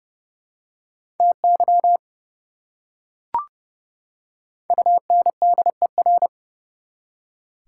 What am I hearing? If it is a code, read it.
Morse code, heard as a keyed tone, about 30 wpm, 700 Hz. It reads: TY UNDER
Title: Prefixes and Suffixes 30wpm